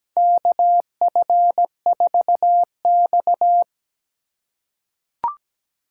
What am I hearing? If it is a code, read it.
Morse code, heard as a keyed tone, about 17 wpm, 700 Hz. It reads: KF4X
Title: USA Call Signs - 4 Characters - 2x Character Spacing - 17wpm